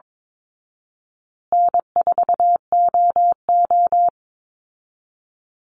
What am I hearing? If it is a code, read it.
Morse code, heard as a keyed tone, about 22 wpm, 690 Hz. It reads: N4OO
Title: Call Signs Repeated 2x - 22wpm